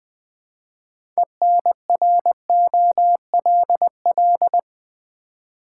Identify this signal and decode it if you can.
Morse code, heard as a keyed tone, about 20 wpm, 700 Hz. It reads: ENROLL
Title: 6 Letter Words - First 3 Letters Given - 20wpm